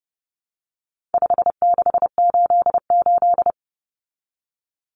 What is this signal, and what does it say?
Morse code, heard as a keyed tone, about 30 wpm, 700 Hz. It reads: 5688
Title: Common Cut Numbers - 4-Digit 30wpm